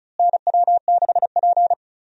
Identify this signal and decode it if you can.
Morse code, heard as a keyed tone, about 36 wpm, 700 Hz. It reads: NW6P
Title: USA Call Signs - 4 Characters - Repeated 3x - 35wpm